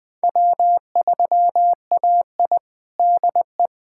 Morse code, heard as a keyed tone, about 20 wpm, 700 Hz. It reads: W3AI DE
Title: USA Callsign DE Callsign - 4 Characters - 20wpm